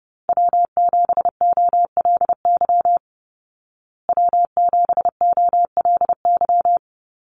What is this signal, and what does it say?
Morse code, heard as a keyed tone, about 30 wpm, 700 Hz. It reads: W7OLY W7OLY
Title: Call Signs Repeated 3x - 30wpm